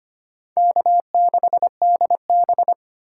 Morse code, heard as a keyed tone, about 25 wpm, 700 Hz. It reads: K6DB
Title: USA Call Signs - 4 Characters - 2x Character Spacing - 25wpm